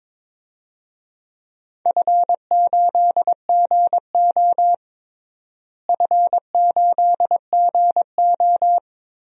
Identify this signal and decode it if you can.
Morse code, heard as a keyed tone, about 22 wpm, 700 Hz. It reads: F8GO F8GO
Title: DXCC-100 Call Signs - 4 Characters - Repeated 3x - 22wpm